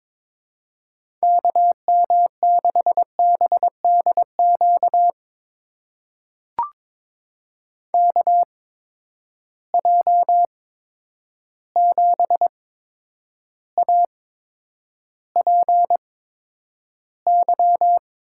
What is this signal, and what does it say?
Morse code, heard as a keyed tone, about 23 wpm, spaced as for 10 wpm, 700 Hz. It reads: KM6BDQ KJ7APY
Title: USA Call Signs - 6 Characters - 8x Character Spacing - 22wpm